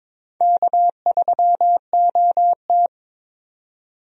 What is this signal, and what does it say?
Morse code, heard as a keyed tone, about 22 wpm, 700 Hz. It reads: K3OT
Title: USA Call Signs - 4 Characters - 22wpm